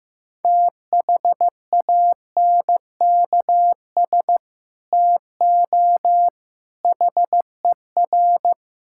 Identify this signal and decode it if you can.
Morse code, heard as a keyed tone, about 15 wpm, 700 Hz. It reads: THANKS TO HER
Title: Sentences from Top 1000 Words - Review Part 5 - 15wpm